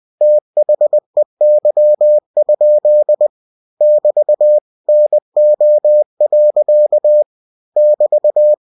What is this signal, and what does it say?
Morse code, heard as a keyed tone, about 20 wpm, 600 Hz. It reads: THEY? <BT> NO. <BT>
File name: cw-books / winnie-the-pooh / winnie-20wpm-page-0011